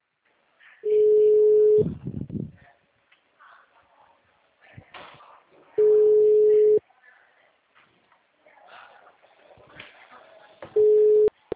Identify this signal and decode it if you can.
Ringback tone (Europe, 425 Hz), with speech.